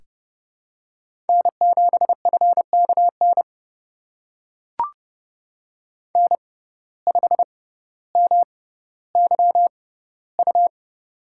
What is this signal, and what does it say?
Morse code, heard as a keyed tone, about 30 wpm, spaced as for 15 wpm, 700 Hz. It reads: N7FKN N5MYU
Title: USA Call Signs - 5 Characters - 6x Character Spacing - 30wpm